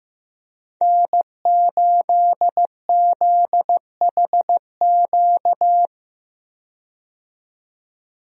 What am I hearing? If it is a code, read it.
Morse code, heard as a keyed tone, about 15 wpm, 700 Hz. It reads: N8ZHQ
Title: USA Call Signs - 5 Characters - 15wpm